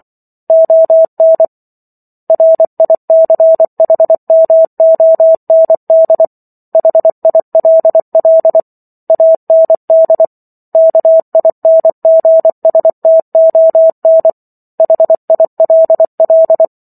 Morse code, heard as a keyed tone, about 24 wpm, 650 Hz. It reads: ON RICHMOND HILL AND KINGSTON HILL